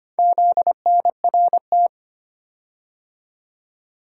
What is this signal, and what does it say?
Morse code, heard as a keyed tone, about 25 wpm, 700 Hz. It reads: ZNRT
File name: practice / 25wpm / Four Letter-Number Combinations - 25wpm